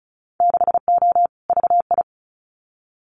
Morse code, heard as a keyed tone, about 36 wpm, 700 Hz. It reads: 6O VI
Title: Winter Field Day Exchanges Encore 35wpm